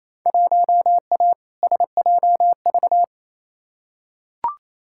Morse code, heard as a keyed tone, about 28 wpm, 700 Hz. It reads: 1A SJV
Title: Field Day Exchanges Encore - 28wpm